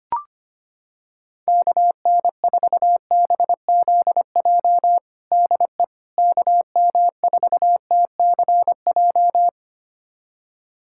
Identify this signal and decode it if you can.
Morse code, heard as a keyed tone, about 25 wpm, 700 Hz. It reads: KN4BZJ DE KM4TCJ
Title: USA Callsign DE Callsign - 6 Characters - 25wpm